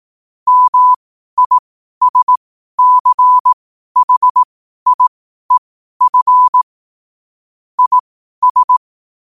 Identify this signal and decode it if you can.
Morse code, heard as a keyed tone, about 18 wpm, spaced as for 12 wpm, 1000 Hz. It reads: MISCHIEF IS
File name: qotdfiles / 12wpm